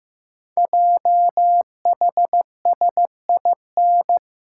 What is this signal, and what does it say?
Morse code, heard as a keyed tone, about 15 wpm, 700 Hz. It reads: JHSIN
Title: Code Groups - 5 Letters - 15wpm